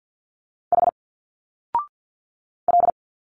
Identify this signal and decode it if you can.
Morse code, heard as a keyed tone, about 50 wpm, 700 Hz. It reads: H L
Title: Single Letters 50wpm